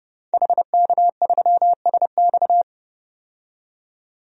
Morse code, heard as a keyed tone, about 30 wpm, 700 Hz. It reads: HK3SX
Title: DXCC-100 Call Signs - 5 Characters - 30wpm